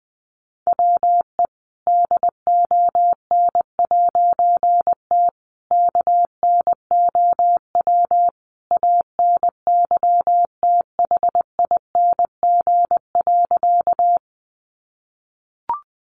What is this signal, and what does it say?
Morse code, heard as a keyed tone, about 20 wpm, 700 Hz. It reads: WE DON'T KNOW ANYTHING.
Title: A Sound Of Thunder 20wpm